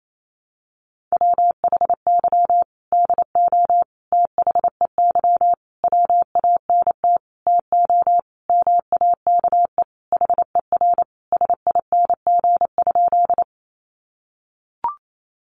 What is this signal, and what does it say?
Morse code, heard as a keyed tone, about 28 wpm, 700 Hz. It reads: WHY DO THEY WANT TO MAKE HER SING?
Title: Sentences from Top 400 Words 28wpm